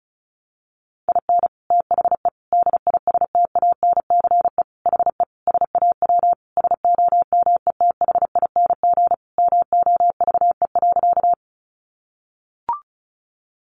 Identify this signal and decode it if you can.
Morse code, heard as a keyed tone, about 34 wpm, 700 Hz. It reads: IN THE DISTANCE HE SAW SOMETHING MOVE.
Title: Sentences from Top 1200 Words 35wpm